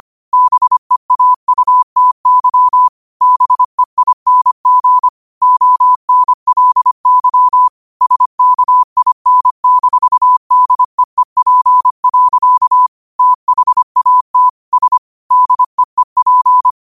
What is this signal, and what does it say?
Morse code, heard as a keyed tone, about 25 wpm, 1000 Hz. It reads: BEAUTY BEING ONLY SKIN-DEEP. THAT S DEEP